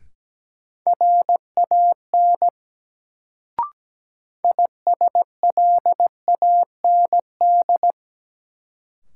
Morse code, heard as a keyed tone, about 17 wpm, 700 Hz. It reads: RAN ISLAND
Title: Sentences from Top 500 Words - New Word Drill - 17wpm